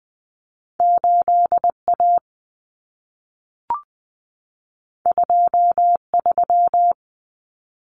Morse code, heard as a keyed tone, about 20 wpm, 700 Hz. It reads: 8A 23
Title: Common Cut Numbers - 2-Digit 20wpm